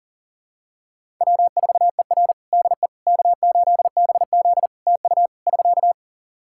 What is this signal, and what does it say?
Morse code, heard as a keyed tone, about 40 wpm, 700 Hz. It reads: W4ER DE K8BZ TU <SK>
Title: Practice QSO - Exchange 4 - Ending - 40wpm